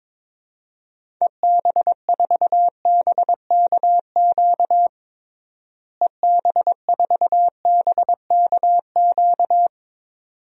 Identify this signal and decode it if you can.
Morse code, heard as a keyed tone, about 22 wpm, 700 Hz. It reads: EB4BKQ EB4BKQ
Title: DXCC-100 Call Signs - 6 Characters - Repeated 2x - 22wpm